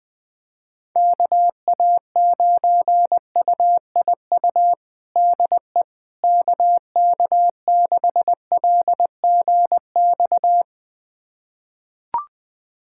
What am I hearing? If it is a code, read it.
Morse code, heard as a keyed tone, about 20 wpm, 700 Hz. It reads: KA9UIU DE KK6LGX
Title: USA Callsign DE Callsign - 6 Characters - 20wpm